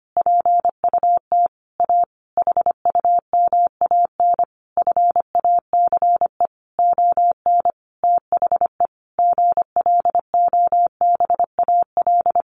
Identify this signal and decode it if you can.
Morse code, heard as a keyed tone, about 25 wpm, 700 Hz. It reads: PUT A HUMAN FACE ON THE GLOBAL
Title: Sentences from Top 1200 Words 25wpm